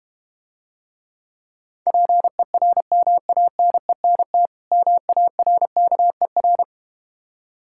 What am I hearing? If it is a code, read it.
Morse code, heard as a keyed tone, about 32 wpm, 690 Hz. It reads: PERMANENT MARKER